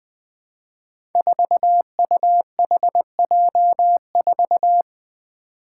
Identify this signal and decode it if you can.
Morse code, heard as a keyed tone, about 20 wpm, 700 Hz. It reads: 4UHJ4
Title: Five Challenging Letter-Number Combinations 20wpm